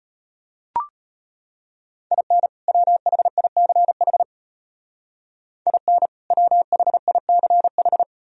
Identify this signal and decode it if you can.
Morse code, heard as a keyed tone, about 30 wpm, 700 Hz. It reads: IN WHICH IN WHICH
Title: Speed-Racing - Sets of 2 Words 25wpm